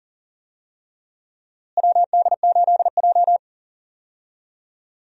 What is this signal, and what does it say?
Morse code, heard as a keyed tone, about 40 wpm, 700 Hz. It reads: WD8J
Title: USA Call Signs - 4 Characters - Repeated 3x - 40wpm